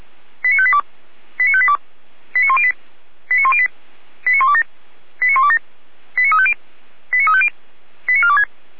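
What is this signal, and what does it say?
Five-tone selective call (ZVEI), heard as quick runs of five tones, tones of about 80 ms. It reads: 89582 89582 89108 89108 89127 89127 89350 89350 89426
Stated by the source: ZVEI five-tone call received on SDR.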